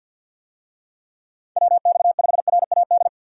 Morse code, heard as a keyed tone, about 46 wpm, 700 Hz. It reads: WX5RAD
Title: USA Call Signs - 6 Characters - Repeated 3x - 50wpm